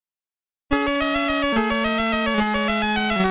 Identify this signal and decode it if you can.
There is no code.